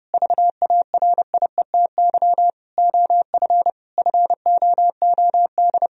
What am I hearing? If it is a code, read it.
Morse code, heard as a keyed tone, about 32 wpm, 700 Hz. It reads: VARIETY OF FOOD
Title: Sentences from Top 1000 Words - Review Part 4 - 30wpm